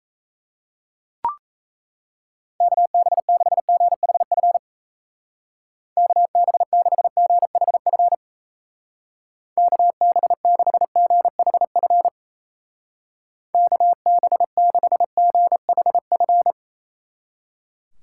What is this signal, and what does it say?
Morse code, heard as a keyed tone, about 35 wpm, 700 Hz. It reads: KB6GHF KB6GHF KB6GHF KB6GHF
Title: Speed Racing - USA 2x3 Call Signs 28wpm